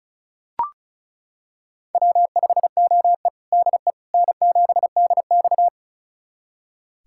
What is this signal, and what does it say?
Morse code, heard as a keyed tone, about 34 wpm, 700 Hz. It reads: W5OE DE N7DX
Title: USA Callsign DE Callsign - 4 Characters - 35wpm